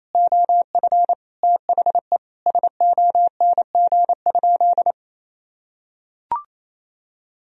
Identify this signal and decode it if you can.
Morse code, heard as a keyed tone, about 28 wpm, 700 Hz. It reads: OF THE SONG?
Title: Sentences from Top 1000 Words - Review Part 2 - 28wpm